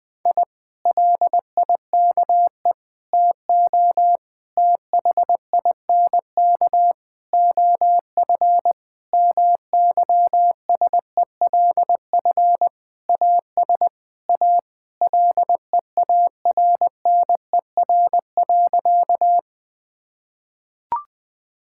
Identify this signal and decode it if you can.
Morse code, heard as a keyed tone, about 21 wpm, 700 Hz. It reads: I LIKE TO THINK OF MYSELF AS A LEARNER.
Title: Sentences with 8-Word Phrases - 20wpm